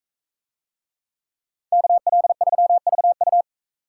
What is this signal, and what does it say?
Morse code, heard as a keyed tone, about 44 wpm, 700 Hz. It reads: KL3VU